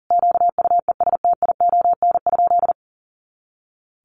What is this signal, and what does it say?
Morse code, heard as a keyed tone, about 37 wpm, 700 Hz. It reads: QUESTION?
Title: Sentences from Top 400 Words 40wpm